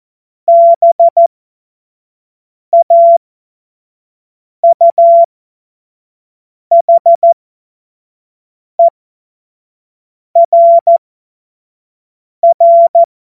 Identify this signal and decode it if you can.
Morse code, heard as a keyed tone, about 14 wpm, spaced as for 5 wpm, 680 Hz. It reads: BAUHERR
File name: Morsekurs / Witze / Witz 08. August 2023 25 F70